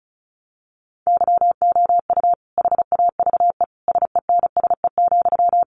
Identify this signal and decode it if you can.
Morse code, heard as a keyed tone, about 34 wpm, 700 Hz. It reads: YOU HAVE SENSE,
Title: Sentences from Top 1100 Words 35wpm